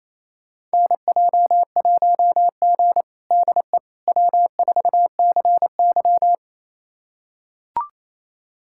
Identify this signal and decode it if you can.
Morse code, heard as a keyed tone, about 28 wpm, 700 Hz. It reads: NJ1G DE W4CY
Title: USA Callsign DE Callsign - 4 Characters - 28wpm